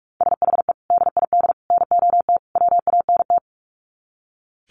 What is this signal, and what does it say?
Morse code, heard as a keyed tone, about 44 wpm, 700 Hz. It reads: SHE DID NOT WANT
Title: Sets of 4 Words 45wpm